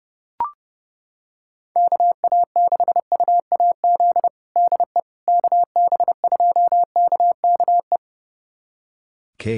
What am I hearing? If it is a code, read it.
Morse code, heard as a keyed tone, about 30 wpm, 700 Hz. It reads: KA6UAZ DE KB2KKE
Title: USA Callsign DE Callsign - 6 Characters - 30wpm